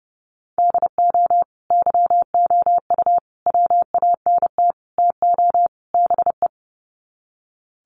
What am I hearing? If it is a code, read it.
Morse code, heard as a keyed tone, about 30 wpm, 700 Hz. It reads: DO YOU WANT TO BE
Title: Sets of 5 Words Encore2 30wpm